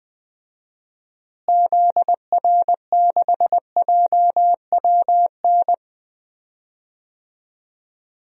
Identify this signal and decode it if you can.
Morse code, heard as a keyed tone, about 21 wpm, 700 Hz. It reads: ZR6JWN
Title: DXCC-100 Call Signs - 6 Characters - Repeated 2x - 20wpm